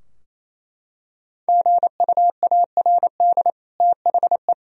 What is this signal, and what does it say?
Morse code, heard as a keyed tone, about 28 wpm, 700 Hz. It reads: GUARD THE
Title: Sentences from Top 1000 Words - Review Part 4 - 28wpm